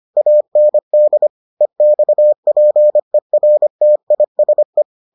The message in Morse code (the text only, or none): AND EXPERTISE